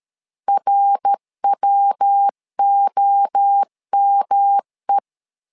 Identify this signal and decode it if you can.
Morse code, heard as a keyed tone, about 13 wpm, 800 Hz. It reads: RWOME
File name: LEZIONI / 4 Q1S3 12wpm 2023